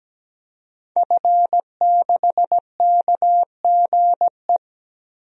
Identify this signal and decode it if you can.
Morse code, heard as a keyed tone, about 17 wpm, 700 Hz. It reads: F6KGE
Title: DXCC-100 Call Signs - 5 Characters - Repeated 2x - 17wpm